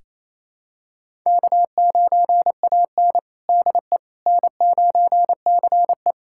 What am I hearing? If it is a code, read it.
Morse code, heard as a keyed tone, about 28 wpm, 700 Hz. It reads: K9AN DE N9CE
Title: USA Callsign DE Callsign - 4 Characters - 28wpm